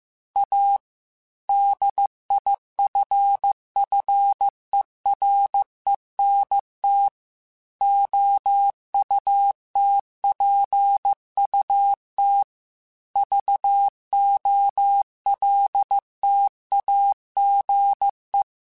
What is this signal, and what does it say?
Morse code, heard as a keyed tone, about 15 wpm, 800 Hz. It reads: A DIFFERENT OUTPUT VOLTAGE
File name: Technical Subjects / Technical Subjects-mp3s / Sw Pwr Sup @13